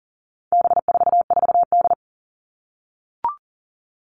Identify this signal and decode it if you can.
Morse code, heard as a keyed tone, about 40 wpm, 700 Hz. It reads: B44D
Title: All Cut Numbers - 4-Digit 40wpm